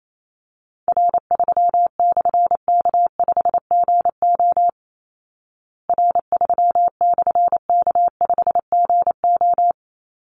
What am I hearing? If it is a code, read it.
Morse code, heard as a keyed tone, about 28 wpm, 700 Hz. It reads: R3/K5GO R3/K5GO
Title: Call Signs Repeated 3x - 28wpm